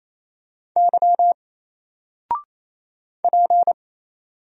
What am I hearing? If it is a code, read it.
Morse code, heard as a keyed tone, about 27 wpm, 700 Hz. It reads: Y P
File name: Single-Letter-Number-28wpm